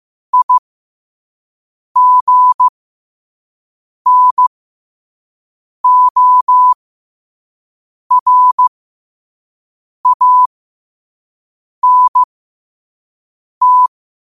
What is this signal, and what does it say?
Morse code, heard as a keyed tone, about 15 wpm, spaced as for 5 wpm, 1000 Hz. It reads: IGNORANT
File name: qotdfiles / 5wpm